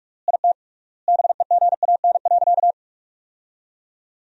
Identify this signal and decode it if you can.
Morse code, heard as a keyed tone, about 46 wpm, 700 Hz. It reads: IT BEGAN.
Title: 3x Word-Spacing - Sentences from Top 400 Words - 45wpm